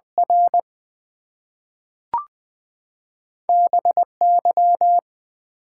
Morse code, heard as a keyed tone, about 20 wpm, 700 Hz. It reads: R BY
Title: Two Letter Words - 20wpm